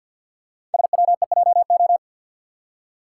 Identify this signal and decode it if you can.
Morse code, heard as a keyed tone, about 51 wpm, 690 Hz. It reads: SWEJO
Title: Code Groups - 5 Letters - 50wpm